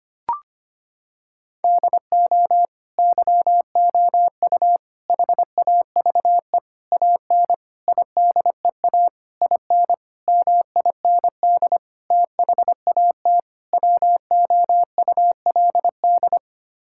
Morse code, heard as a keyed tone, about 25 wpm, 700 Hz. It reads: DO YOU HAVE AN IDEA IN MIND THAT WOULD
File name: Sets of 9 Words - 25wpm